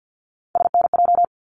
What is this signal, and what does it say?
Morse code, heard as a keyed tone, about 50 wpm, 700 Hz. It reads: SNJ
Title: ARRL Sections 50wpm